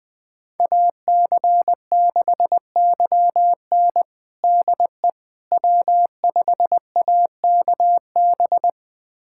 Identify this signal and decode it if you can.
Morse code, heard as a keyed tone, about 21 wpm, 700 Hz. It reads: AC6YN DE W5AKB
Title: USA Callsign DE Callsign - 5 Characters - 20wpm